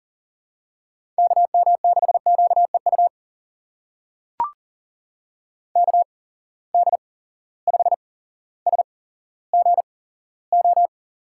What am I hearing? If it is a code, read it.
Morse code, heard as a keyed tone, about 40 wpm, spaced as for 17 wpm, 700 Hz. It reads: KM6QEU KD5SGO
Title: USA Call Signs - 6 Characters - 8x Character Spacing - 40wpm